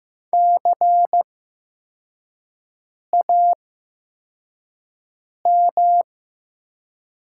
Morse code, heard as a keyed tone, about 15 wpm, spaced as for 4 wpm, 700 Hz. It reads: CAM
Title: 8x Character Spacing - Top 300 Words - 15wpm